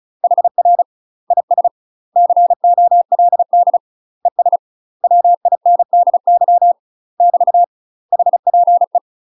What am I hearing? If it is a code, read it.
Morse code, heard as a keyed tone, about 34 wpm, 700 Hz. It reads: HR IS COLD ES WINDY <BT> HPE
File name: UK MASTER QSOs TRAINING / 35WPM / UK MASTER QSOs 0043